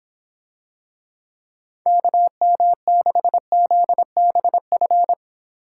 Morse code, heard as a keyed tone, about 25 wpm, 700 Hz. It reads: KM6ZBF